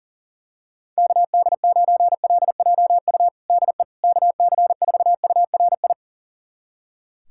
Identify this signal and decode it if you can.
Morse code, heard as a keyed tone, about 37 wpm, 700 Hz. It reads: KD9LJU DE KC4URI
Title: USA Callsign DE Callsign 40wpm